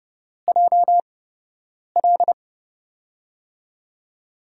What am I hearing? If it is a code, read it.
Morse code, heard as a keyed tone, about 30 wpm, spaced as for 8 wpm, 700 Hz. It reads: JL
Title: USA Call Signs - 6 Characters - 8x Character Spacing - 30wpm